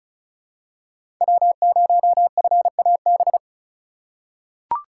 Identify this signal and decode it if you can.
Morse code, heard as a keyed tone, about 36 wpm, 700 Hz. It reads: W0FAB
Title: USA Call Signs - 5 Characters - 3x Character Spacing - 35wpm